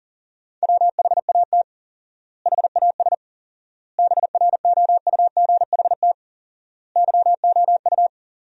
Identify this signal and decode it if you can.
Morse code, heard as a keyed tone, about 40 wpm, 700 Hz. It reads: WHAT HAS BROUGHT YOU
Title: 4x Word-Spacing - Sentences from Top 500 Words - 40wpm